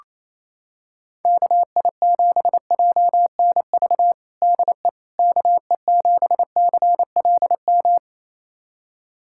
Morse code, heard as a keyed tone, about 27 wpm, 700 Hz. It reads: KI7JNV DE KE7CLM
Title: USA Callsign DE Callsign - 6 Characters - 28wpm